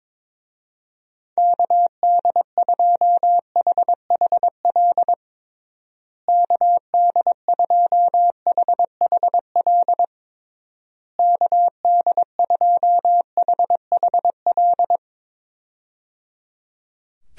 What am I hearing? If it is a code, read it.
Morse code, heard as a keyed tone, about 22 wpm, 700 Hz. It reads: KD2HHL KD2HHL KD2HHL
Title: USA Call Signs - 6 Characters - Repeated 3x - 22wpm